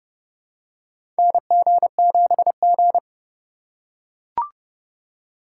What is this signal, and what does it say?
Morse code, heard as a keyed tone, about 32 wpm, 700 Hz. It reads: NG7G
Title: USA Call Signs - 4 Characters - 3x Character Spacing - 30wpm